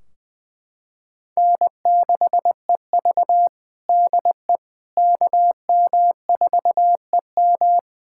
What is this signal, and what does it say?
Morse code, heard as a keyed tone, about 20 wpm, 700 Hz. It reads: N6EV DE KM4EM
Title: Practice QSO - Exchange 4 - Ending - 20wpm